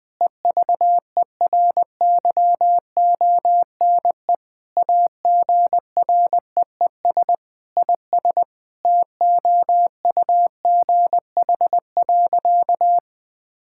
Morse code, heard as a keyed tone, about 20 wpm, 700 Hz. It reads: EVERYONE AGREES IS TOUGH.
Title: Sentences with 7-Word Phrases - 20wpm